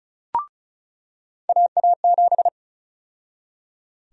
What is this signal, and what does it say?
Morse code, heard as a keyed tone, about 36 wpm, 700 Hz. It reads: AA7
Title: USA Call Sign Prefixes - 35wpm